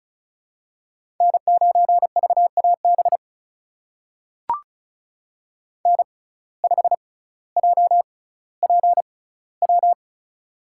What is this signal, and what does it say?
Morse code, heard as a keyed tone, about 36 wpm, spaced as for 18 wpm, 700 Hz. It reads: N9VAB N5JPW
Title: USA Call Signs - 5 Characters - 6x Character Spacing - 35wpm